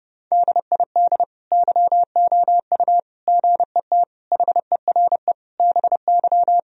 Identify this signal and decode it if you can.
Morse code, heard as a keyed tone, about 30 wpm, 700 Hz. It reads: DID YOU GET HERE BY
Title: Sentences from Top 1000 Words - Review Part 3 - 30wpm